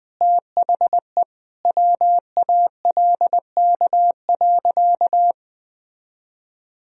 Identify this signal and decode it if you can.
Morse code, heard as a keyed tone, about 20 wpm, 700 Hz. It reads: THE WALK.
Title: Sentences from Top 1000 Words - Review Part 2 - 20wpm